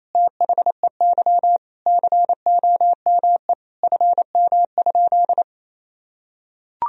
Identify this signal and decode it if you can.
Morse code, heard as a keyed tone, about 28 wpm, 700 Hz. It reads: THEY COME FM?
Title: Sentences from Top 200 Words and Abbreviations 28wpm